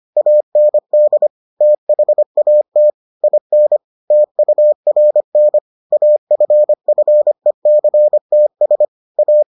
Morse code, heard as a keyed tone, about 25 wpm, 600 Hz. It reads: AND THAT IN TURN AFFECTS A